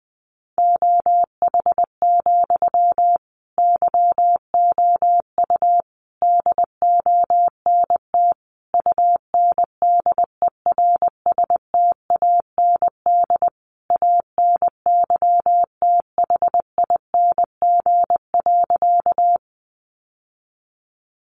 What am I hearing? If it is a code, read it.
Morse code, heard as a keyed tone, about 20 wpm, 700 Hz. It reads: OH, YOU DONT UNDERSTAND ANYTHING.
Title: Sentences from Top 1000 Words 20wpm